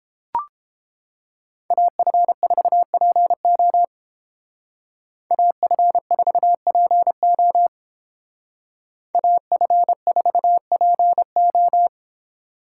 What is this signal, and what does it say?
Morse code, heard as a keyed tone, about 31 wpm, 700 Hz. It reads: AF4PO AF4PO AF4PO